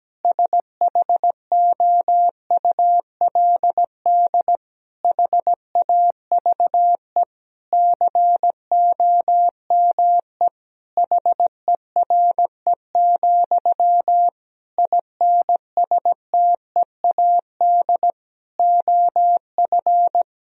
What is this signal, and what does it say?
Morse code, heard as a keyed tone, about 17 wpm, 700 Hz. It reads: SHOULD HAVE COME HERE, INSTEAD OF
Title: Sentences from Top 1000 Words - Review Part 4 - 17wpm